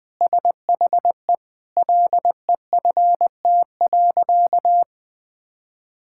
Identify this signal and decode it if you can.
Morse code, heard as a keyed tone, about 20 wpm, 700 Hz. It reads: SHE LEFT.
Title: Sentences with 4-Word Phrases - 20wpm